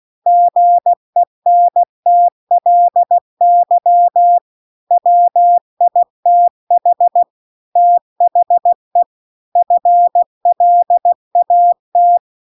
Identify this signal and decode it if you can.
Morse code, heard as a keyed tone, about 16 wpm, 700 Hz. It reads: GENTLY WITH THE FLAT